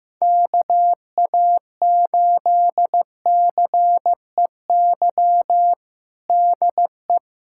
Morse code, heard as a keyed tone, about 15 wpm, 700 Hz. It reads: KA8CEY DE
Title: USA Callsign DE Callsign - 6 Characters - 15wpm